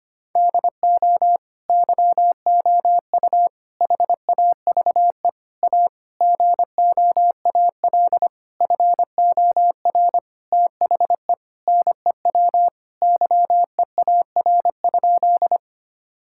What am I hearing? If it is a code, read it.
Morse code, heard as a keyed tone, about 25 wpm, 700 Hz. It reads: DO YOU HAVE A GOAL FOR THE NEW YEAR?
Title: Sentences from Top 1000 Words - Review Part 5 - 25wpm